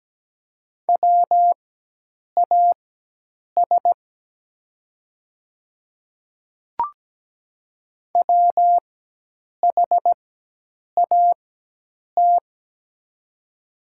Morse code, heard as a keyed tone, about 17 wpm, spaced as for 8 wpm, 700 Hz. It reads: WAS WHAT
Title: 4x Character Spacing - Top 100 Words - 17wpm